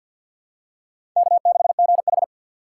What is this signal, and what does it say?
Morse code, heard as a keyed tone, about 53 wpm, 700 Hz. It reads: K6GH
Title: USA Call Signs - 4 Characters - 2x Character Spacing - 50wpm